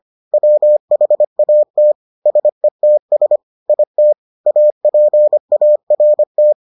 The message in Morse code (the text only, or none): WHAT SETS IT APART